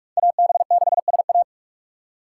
Morse code, heard as a keyed tone, about 46 wpm, 700 Hz. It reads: AB6SA
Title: USA Call Signs - 5 Characters - 6x Character Spacing - 45wpm